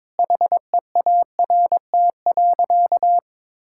Morse code, heard as a keyed tone, about 22 wpm, 700 Hz. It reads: HEART.
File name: Sentences with 7-Word Phrases - 22wpm